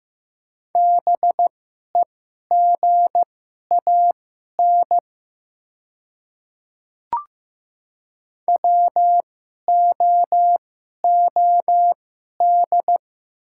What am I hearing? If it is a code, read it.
Morse code, heard as a keyed tone, about 15 wpm, spaced as for 11 wpm, 690 Hz. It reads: BEGAN WOOD
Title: 2x Character Spacing - Top 300 Words - 15wpm